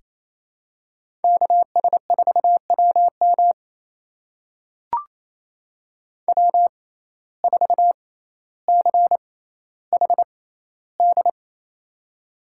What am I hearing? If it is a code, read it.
Morse code, heard as a keyed tone, about 27 wpm, spaced as for 14 wpm, 700 Hz. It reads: KS4WM W4CHD
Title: USA Call Signs - 5 Characters - 6x Character Spacing - 28wpm